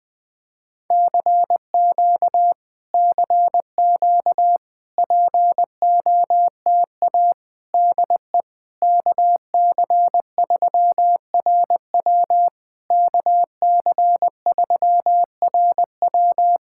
Morse code, heard as a keyed tone, about 20 wpm, 690 Hz. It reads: CQ CQ POTA DE KC3RW KC3RW
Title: POTA QSO Practice - 20wpm